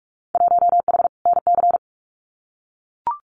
Morse code, heard as a keyed tone, about 44 wpm, 700 Hz. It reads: 1H NC
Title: Winter Field Day Exchanges 45wpm